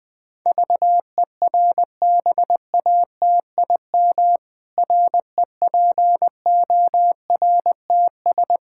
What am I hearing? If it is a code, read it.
Morse code, heard as a keyed tone, about 20 wpm, 700 Hz. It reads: VERBATIM REPORTS
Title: Pangram Challenge v2 - 20wpm